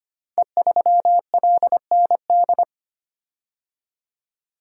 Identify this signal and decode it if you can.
Morse code, heard as a keyed tone, about 25 wpm, 700 Hz. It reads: E3LND
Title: Code Groups - 5 Letter:Numbers - 25wpm